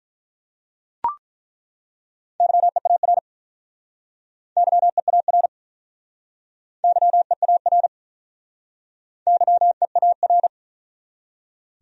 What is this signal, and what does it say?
Morse code, heard as a keyed tone, about 47 wpm, 700 Hz. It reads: YEAR YEAR YEAR YEAR